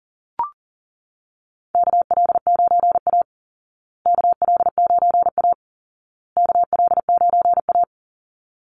Morse code, heard as a keyed tone, about 40 wpm, 700 Hz. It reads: KL9A KL9A KL9A
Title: Call Signs Repeated 2x - 40wpm